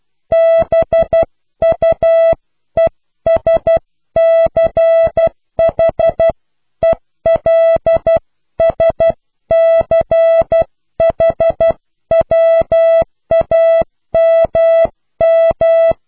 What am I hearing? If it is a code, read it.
Morse code, heard as a keyed tone, about 12 wpm, 660 Hz. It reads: BUESCHELSCHWAMM
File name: cw / Pseudo woorden (15)